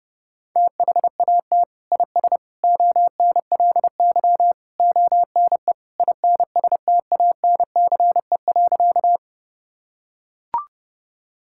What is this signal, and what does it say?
Morse code, heard as a keyed tone, about 30 wpm, 700 Hz. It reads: THAT IS ONLY ONE INSTANCE.
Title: Sentences from Top 1000 Words - Review Part 5 - 30wpm